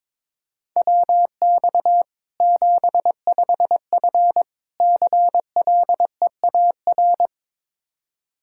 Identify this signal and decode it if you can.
Morse code, heard as a keyed tone, about 22 wpm, 700 Hz. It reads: WX 75F CLEAR
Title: Practice QSO - Exchange 2 - Info - 22wpm